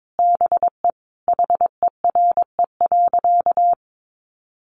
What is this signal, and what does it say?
Morse code, heard as a keyed tone, about 22 wpm, 700 Hz. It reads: BE HERE.
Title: Sentences from Top 1200 Words 22wpm